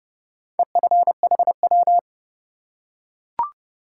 Morse code, heard as a keyed tone, about 30 wpm, 700 Hz. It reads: EFHW
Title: Top 180 QSO Elements 30wpm